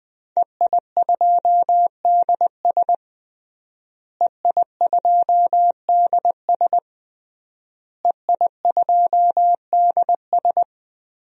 Morse code, heard as a keyed tone, about 20 wpm, 700 Hz. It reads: EI2DS EI2DS EI2DS
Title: DXCC-100 Call Signs - 5 Characters - Repeated 3x - 20wpm